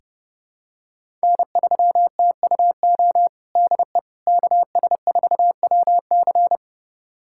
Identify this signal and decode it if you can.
Morse code, heard as a keyed tone, about 30 wpm, 700 Hz. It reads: N3TUO DE KS4WC
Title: USA Callsign DE Callsign - 5 Characters - 30wpm